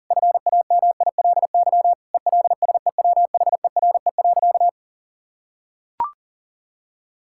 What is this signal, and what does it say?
Morse code, heard as a keyed tone, about 40 wpm, 700 Hz. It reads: FAMILY ELSEWHERE.
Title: Sentences from Top 1600 Words 40wpm